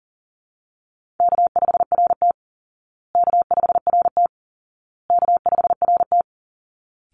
Morse code, heard as a keyed tone, about 40 wpm, 700 Hz. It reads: K5RT K5RT K5RT
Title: Easy Call Signs Repeated 3x - 40wpm